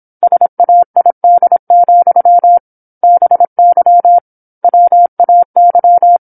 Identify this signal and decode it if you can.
Morse code, heard as a keyed tone, about 27 wpm, 700 Hz. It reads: SAID, BY WAY
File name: 08-Chapter-08 26 WPM